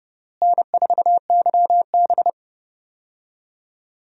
Morse code, heard as a keyed tone, about 30 wpm, 700 Hz. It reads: N4YB
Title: USA Call Signs - 4 Characters - Repeated 3x - 30wpm